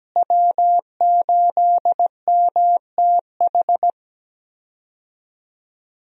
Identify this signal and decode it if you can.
Morse code, heard as a keyed tone, about 17 wpm, 700 Hz. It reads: W8MTH
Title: USA Call Signs - 5 Characters - Repeated 2x - 17wpm